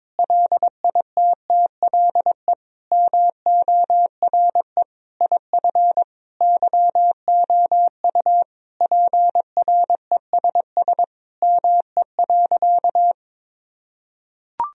Morse code, heard as a keyed tone, about 22 wpm, 700 Hz. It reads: LITTLE MORE IF YOU PRESS ME.
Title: Sentences from Top 1000 Words - Review Part 1 - 22wpm